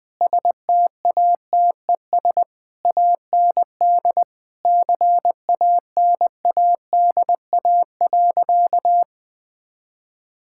Morse code, heard as a keyed tone, about 20 wpm, 700 Hz. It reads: STATES AND CANADA.
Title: Sentences with 5-Word Phrases - Spoken Beginning - 20wpm